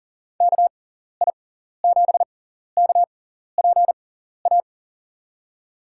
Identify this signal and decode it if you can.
Morse code, heard as a keyed tone, about 37 wpm, spaced as for 14 wpm, 700 Hz. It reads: KI7KPA
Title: USA Call Signs - 6x Character Spacing - 40wpm